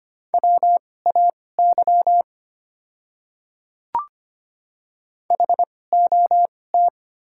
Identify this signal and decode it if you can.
Morse code, heard as a keyed tone, about 25 wpm, spaced as for 18 wpm, 700 Hz. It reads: WAY HOT
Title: 2x Character Spacing - Top 100 Words - 25wpm